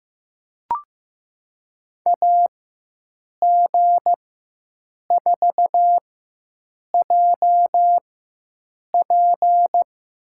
Morse code, heard as a keyed tone, about 15 wpm, spaced as for 7 wpm, 700 Hz. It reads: AG4JP
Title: USA Call Signs - 5 Characters - 4x Character Spacing - 15wpm